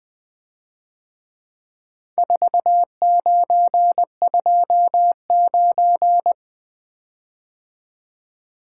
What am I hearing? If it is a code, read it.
Morse code, heard as a keyed tone, about 24 wpm, 700 Hz. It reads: 4929
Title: Speed-Racing - Four Letter-Number Combinations - 20wpm